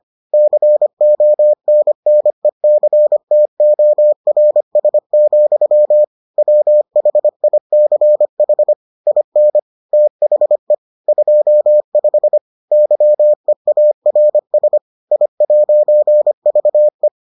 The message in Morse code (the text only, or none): CONNECTORS, WHICH IN THE 25 YEARS I'VE